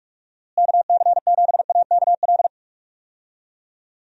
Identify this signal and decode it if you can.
Morse code, heard as a keyed tone, about 43 wpm, 700 Hz. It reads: KC7AKL
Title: USA Call Signs - 6 Characters - Repeated 2x - 45wpm